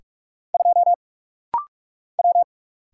Morse code, heard as a keyed tone, about 44 wpm, 700 Hz. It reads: 2 W
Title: Single Letter-Number - Rapid-Fire 45wpm